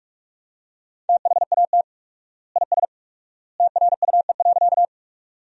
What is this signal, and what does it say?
Morse code, heard as a keyed tone, about 46 wpm, 700 Hz. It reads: THAT IS TRUE.
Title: 4x Word-Spacing - Sentences from Top 400 Words - 45wpm